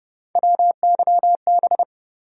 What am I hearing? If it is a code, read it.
Morse code, heard as a keyed tone, about 30 wpm, 700 Hz. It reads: WYB
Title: USA Call Sign Postfixes - 3 Letters 30wpm